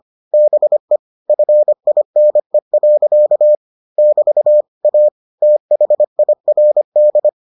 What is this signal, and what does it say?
Morse code, heard as a keyed tone, about 25 wpm, 600 Hz. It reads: BE FINE. <BT> A THIRD